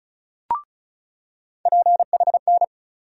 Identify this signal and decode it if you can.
Morse code, heard as a keyed tone, about 36 wpm, 700 Hz. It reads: PHN
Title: USA Call Sign Postfixes - 3 Letters 35wpm